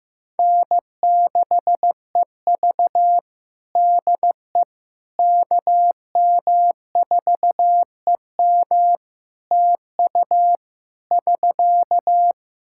Morse code, heard as a keyed tone, about 14 wpm, 700 Hz. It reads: N6EV DE KM4EM TU <SK>
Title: Practice QSO - Exchange 4 - Ending - 15wpm